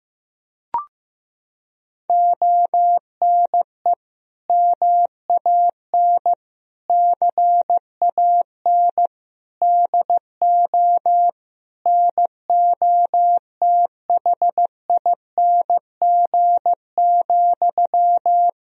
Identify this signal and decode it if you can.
Morse code, heard as a keyed tone, about 15 wpm, 700 Hz. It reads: ONE MAN CAN DO NOTHING,
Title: Sentences from Top 1000 Words - Review Part 4 - 15wpm